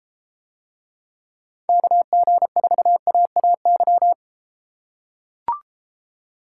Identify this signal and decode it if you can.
Morse code, heard as a keyed tone, about 33 wpm, 700 Hz. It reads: KG4AAY